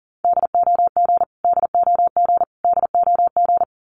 Morse code, heard as a keyed tone, about 40 wpm, 700 Hz. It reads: DOG DOG DOG
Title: Top 500 Words Repeated 40wpm